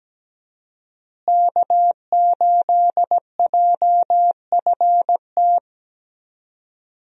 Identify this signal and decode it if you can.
Morse code, heard as a keyed tone, about 17 wpm, 700 Hz. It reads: K8JFT
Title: USA Call Signs - 5 Characters - Repeated 2x - 17wpm